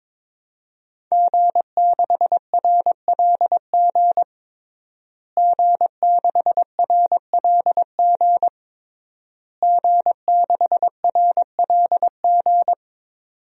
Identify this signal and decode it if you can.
Morse code, heard as a keyed tone, about 22 wpm, 700 Hz. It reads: G6RLG G6RLG G6RLG
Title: DXCC-100 Call Signs - 5 Characters - Repeated 3x - 22wpm